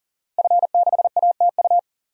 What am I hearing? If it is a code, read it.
Morse code, heard as a keyed tone, about 40 wpm, 700 Hz. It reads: F6ATU
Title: DXCC-100 Call Signs - 5 Characters - Repeated 3x - 40wpm